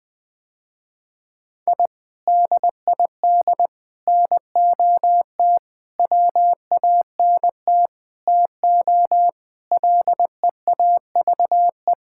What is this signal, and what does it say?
Morse code, heard as a keyed tone, about 21 wpm, 700 Hz. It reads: I DID NOT WANT TO LEAVE
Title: Sentences with 5-Word Phrases - Spoken Beginning - 20wpm